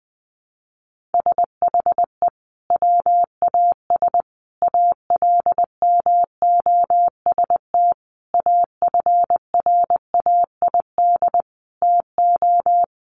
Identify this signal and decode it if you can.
Morse code, heard as a keyed tone, about 20 wpm, 700 Hz. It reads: SHE WAS ALMOST AFRAID TO
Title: Sentences from Top 1500 Words 20wpm